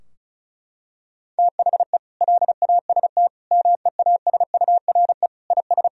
Morse code, heard as a keyed tone, about 36 wpm, 700 Hz. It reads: THE LAST MEASURE IS
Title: Sentences from Top 1000 Words - Review Part 2 - 35wpm